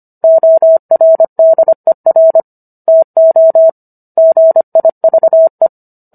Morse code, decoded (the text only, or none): ORDER TO GIVE